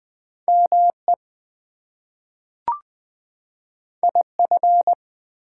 Morse code, heard as a keyed tone, about 20 wpm, 700 Hz. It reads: ME IF
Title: Two Letter Words - 20wpm